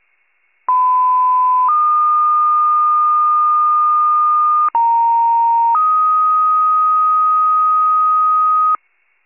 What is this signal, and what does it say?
Two-tone page, heard as a one-second tone, then a three-second tone, about 1010 then 1220 Hz, about 930 then 1250 Hz. It is heard from 0.7 to 8.8 s.
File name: wcalert